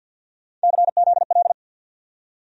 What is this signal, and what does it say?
Morse code, heard as a keyed tone, about 47 wpm, 700 Hz. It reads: CZL
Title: USA Call Sign Postfixes - 3 Letters 50wpm